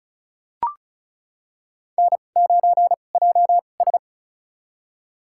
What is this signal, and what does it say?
Morse code, heard as a keyed tone, about 36 wpm, spaced as for 25 wpm, 700 Hz. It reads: N9JS
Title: USA Call Signs - 4 Characters - 2x Character Spacing - 35wpm